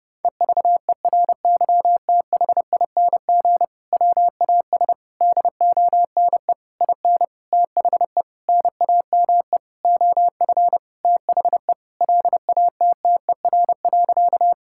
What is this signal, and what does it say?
Morse code, heard as a keyed tone, about 30 wpm, 700 Hz. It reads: EVERYTHING WAS DONE IN THE NAME OF THE LATTER.
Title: Sentences from Top 1600 Words 30wpm